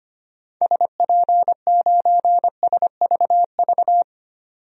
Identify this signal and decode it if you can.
Morse code, heard as a keyed tone, about 25 wpm, 700 Hz. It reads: SP9SVV
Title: DXCC-100 Call Signs - 6 Characters - 25wpm